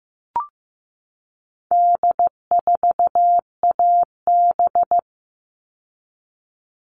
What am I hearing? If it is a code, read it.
Morse code, heard as a keyed tone, about 15 wpm, 700 Hz. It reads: D4AB
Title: All Cut Numbers - 4-Digit 15wpm